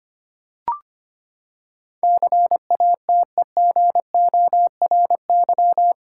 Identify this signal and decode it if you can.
Morse code, heard as a keyed tone, about 25 wpm, 700 Hz. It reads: CATEGORY
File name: Eight Letter Words - 25wpm